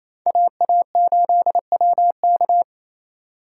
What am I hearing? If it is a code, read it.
Morse code, heard as a keyed tone, about 36 wpm, 700 Hz. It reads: AA8WK